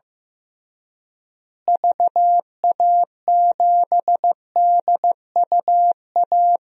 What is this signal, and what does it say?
Morse code, heard as a keyed tone, about 15 wpm, 700 Hz. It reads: VA7DUA
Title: DXCC-100 Call Signs - 6 Characters - Repeated 2x - 15wpm